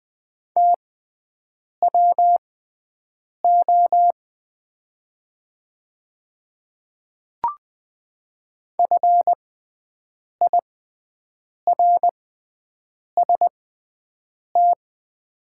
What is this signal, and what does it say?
Morse code, heard as a keyed tone, about 20 wpm, spaced as for 7 wpm, 700 Hz. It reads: TWO FIRST
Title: 6x Character Spacing - Top 100 Words - 20wpm